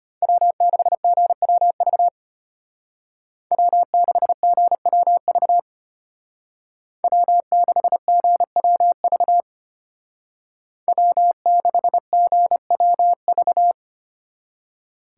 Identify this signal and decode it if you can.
Morse code, heard as a keyed tone, about 30 wpm, 700 Hz. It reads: W6GWV W6GWV W6GWV W6GWV
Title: Speed Racing - USA 1x3 Call Signs 25wpm